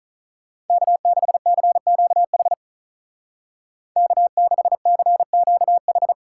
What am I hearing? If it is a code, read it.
Morse code, heard as a keyed tone, about 47 wpm, 700 Hz. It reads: K6CQH K6CQH